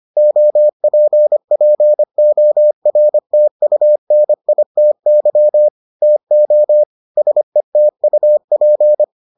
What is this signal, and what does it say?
Morse code, heard as a keyed tone, about 25 wpm, 600 Hz. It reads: OPPORTUNITY TO SETUP